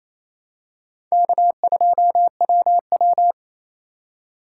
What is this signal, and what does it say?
Morse code, heard as a keyed tone, about 28 wpm, 700 Hz. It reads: K2WW
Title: USA Call Signs - 4 Characters - 3x Character Spacing - 28wpm